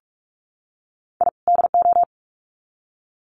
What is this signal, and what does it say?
Morse code, heard as a keyed tone, about 44 wpm, 700 Hz. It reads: I DO
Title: Sets of 2 Words 45wpm